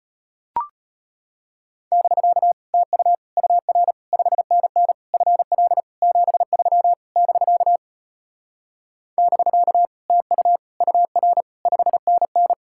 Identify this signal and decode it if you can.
Morse code, heard as a keyed tone, about 33 wpm, 700 Hz. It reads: <BK> TU UR 5NN FL 73 <BK> <BK> TU UR 5NN